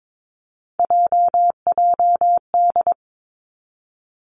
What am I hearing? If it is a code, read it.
Morse code, heard as a keyed tone, about 22 wpm, 700 Hz. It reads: JJD
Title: Three Challenging Letter-Number Combinations 22wpm